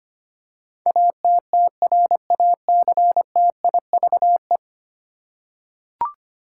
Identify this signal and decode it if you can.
Morse code, heard as a keyed tone, about 25 wpm, 700 Hz. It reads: ATTRACTIVE
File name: 10 Letter Words - First 3 Letters Given - 25wpm